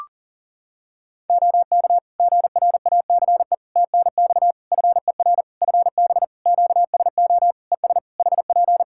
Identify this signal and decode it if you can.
Morse code, heard as a keyed tone, about 39 wpm, 700 Hz. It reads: OK GRACE TNX FER FB QSO ES HP
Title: Practice QSO - Exchange 4 - Ending - 40wpm